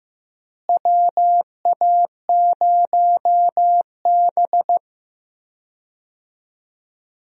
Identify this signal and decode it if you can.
Morse code, heard as a keyed tone, about 15 wpm, 700 Hz. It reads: WA0B
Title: USA Call Signs - 4 Characters - Repeated 3x - 15wpm